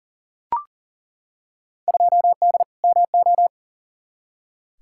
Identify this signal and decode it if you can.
Morse code, heard as a keyed tone, about 40 wpm, 700 Hz. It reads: 2D MO
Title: Field Day Exchanges - 40wpm